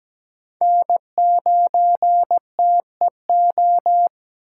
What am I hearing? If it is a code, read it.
Morse code, heard as a keyed tone, about 17 wpm, 700 Hz. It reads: N9TEO
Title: USA Call Signs - 5 Characters - 4x Character Spacing - 17wpm